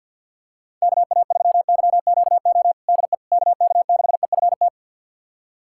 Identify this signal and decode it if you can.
Morse code, heard as a keyed tone, about 49 wpm, 690 Hz. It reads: KA3YQO DE KK6EFT
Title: USA Callsign DE Callsign - 6 Characters - 50wpm